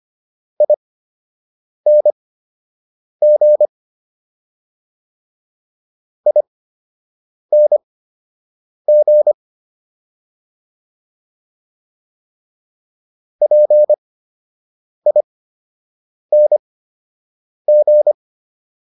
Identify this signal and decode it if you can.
Morse code, heard as a keyed tone, about 25 wpm, spaced as for 7 wpm, 600 Hz. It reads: ING ING PING